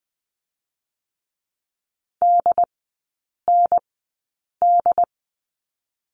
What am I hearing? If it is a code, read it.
Morse code, heard as a keyed tone, about 20 wpm, spaced as for 8 wpm, 700 Hz. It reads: DND
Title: 067 Lesson 12e - Letter d - Three Characters 20wpm